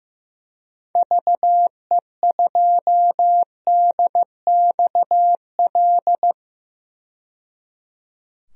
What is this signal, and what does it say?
Morse code, heard as a keyed tone, about 15 wpm, 690 Hz. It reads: VE2DXL
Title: DXCC-100 Call Signs - 6 Characters - 15wpm